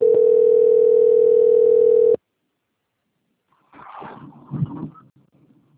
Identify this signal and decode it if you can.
Ringback tone (North America), 440 + 480 Hz.